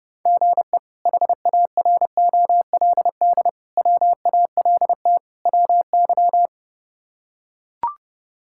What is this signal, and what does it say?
Morse code, heard as a keyed tone, about 30 wpm, 700 Hz. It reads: GE HAROLD WALT WY
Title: K1USN SST QSO Practice - 30wpm